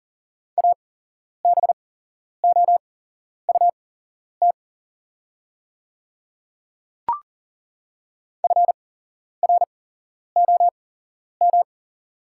Morse code, heard as a keyed tone, about 40 wpm, spaced as for 11 wpm, 700 Hz. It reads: ABOUT FROM
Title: 8x Character Spacing - Top 300 Words - 40wpm